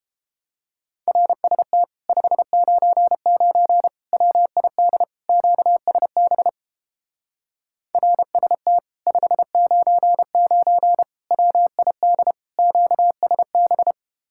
Morse code, heard as a keyed tone, about 31 wpm, 700 Hz. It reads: RST 599 WID QSB RST 599 WID QSB